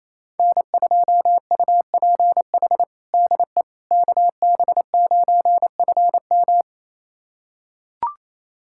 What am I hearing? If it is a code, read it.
Morse code, heard as a keyed tone, about 28 wpm, 700 Hz. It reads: N2UPH DE KB9FM
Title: USA Callsign DE Callsign - 5 Characters - 28wpm